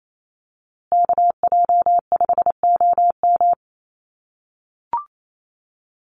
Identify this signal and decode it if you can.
Morse code, heard as a keyed tone, about 28 wpm, 700 Hz. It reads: KJ5OM
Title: USA Call Signs Repeated 2x 28wpm